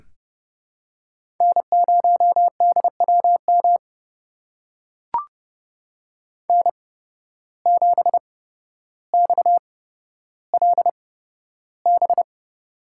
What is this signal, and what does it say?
Morse code, heard as a keyed tone, about 30 wpm, spaced as for 13 wpm, 700 Hz. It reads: N0DWM N7XLB
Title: USA Call Signs - 5 Characters - 8x Character Spacing - 30wpm